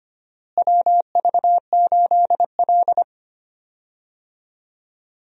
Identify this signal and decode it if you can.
Morse code, heard as a keyed tone, about 25 wpm, 700 Hz. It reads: WV8L
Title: USA Call Signs - 4 Characters - Repeated 2x - 25wpm